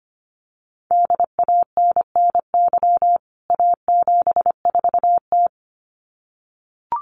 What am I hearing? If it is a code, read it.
Morse code, heard as a keyed tone, about 25 wpm, 700 Hz. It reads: DANNY A74T
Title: CWT Exchange 25wpm